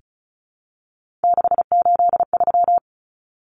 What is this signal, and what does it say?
Morse code, heard as a keyed tone, about 34 wpm, 700 Hz. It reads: 683
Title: Numbers - 3-Digit 35wpm